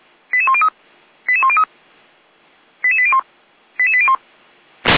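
Five-tone selective call (ZVEI), heard as quick runs of five tones, tones of about 70 ms. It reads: 88103 88103 88801 88801
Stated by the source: Five-tone ZVEI selective call received by SDR.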